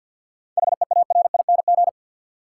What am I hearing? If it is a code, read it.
Morse code, heard as a keyed tone, about 53 wpm, 700 Hz. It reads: HEARING
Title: Seven Letter Words - 50wpm